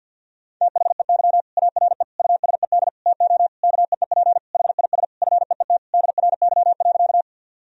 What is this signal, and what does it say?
Morse code, heard as a keyed tone, about 52 wpm, 700 Hz. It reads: THEY ARE USED TO KEEP HIS FEET DRY.
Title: Sentences from Top 1000 Words - Review Part 3 - 50wpm